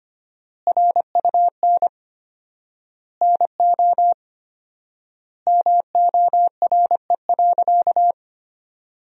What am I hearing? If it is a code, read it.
Morse code, heard as a keyed tone, about 25 wpm, 700 Hz. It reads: RUN NO MORE.
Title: 4x Word-Spacing - Sentences from Top 300 Words - 25wpm